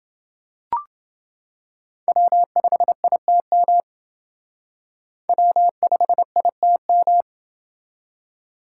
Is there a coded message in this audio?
Yes: Morse code from 2.1 to 7.2 s, heard as a keyed tone, about 28 wpm, 700 Hz. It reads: W5ITM W5ITM